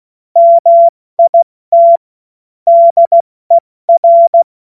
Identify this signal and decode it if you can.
Morse code, heard as a keyed tone, about 16 wpm, 680 Hz. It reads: MIT DER
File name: Morsekurs / Witze / Witz 31. Mai 2022 70 F80